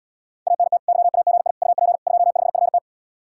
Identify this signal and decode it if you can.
Morse code, heard as a keyed tone, about 79 wpm, 700 Hz. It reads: ANT YAGI UP 25FT